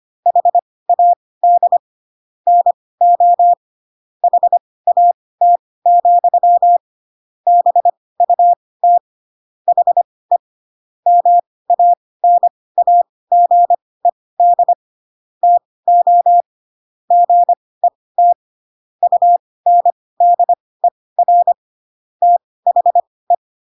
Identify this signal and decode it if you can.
Morse code, heard as a keyed tone, about 25 wpm, spaced as for 18 wpm, 700 Hz. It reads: HAD NO HAT, BUT HE MANAGED TO GET UNDER THE
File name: Pinocchio / Pinocchio ch6 12 25 ik1hge